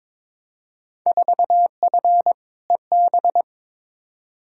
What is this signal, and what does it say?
Morse code, heard as a keyed tone, about 21 wpm, 700 Hz. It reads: 4F EB
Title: Field Day Exchanges - 22wpm